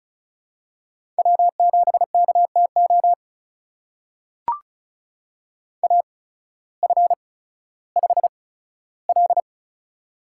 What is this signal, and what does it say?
Morse code, heard as a keyed tone, about 36 wpm, spaced as for 16 wpm, 700 Hz. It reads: W7KTO AF5L
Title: USA Call Signs - 8x Character Spacing - 35wpm